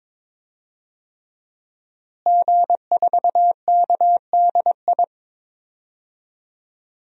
Morse code, heard as a keyed tone, about 22 wpm, 700 Hz. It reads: G4KDI
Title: DXCC-100 Call Signs - 5 Characters - Repeated 3x - 22wpm